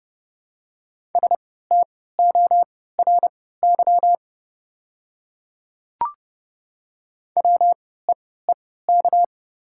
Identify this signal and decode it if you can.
Morse code, heard as a keyed tone, about 30 wpm, spaced as for 17 wpm, 700 Hz. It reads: STORY WEEK
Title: 3x Character Spacing - Top 500 Words - 30wpm